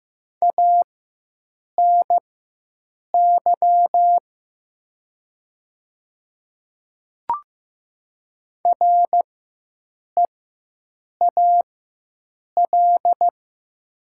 Morse code, heard as a keyed tone, about 15 wpm, spaced as for 7 wpm, 700 Hz. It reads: ANY REAL
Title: 4x Character Spacing - Top 500 Words - 15wpm